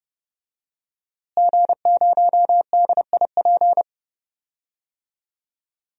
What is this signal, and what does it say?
Morse code, heard as a keyed tone, about 30 wpm, 700 Hz. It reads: G0DIP
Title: DXCC-100 Call Signs - 5 Characters - Repeated 2x - 30wpm